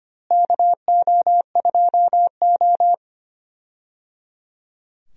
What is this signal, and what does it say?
Morse code, heard as a keyed tone, about 25 wpm, 700 Hz. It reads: KO2O
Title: USA Call Signs - 4 Characters - Repeated 3x - 25wpm